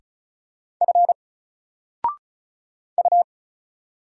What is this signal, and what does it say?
Morse code, heard as a keyed tone, about 36 wpm, 700 Hz. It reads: F U
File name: Single-Letter-Number-35wpm